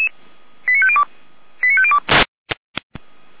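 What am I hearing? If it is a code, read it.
Five-tone selective call (ZVEI), heard as quick runs of five tones, tones of about 70 ms. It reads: 89502 89502
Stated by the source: ZVEI five-tone call received by SDR, Feuerwehr Orlamünde.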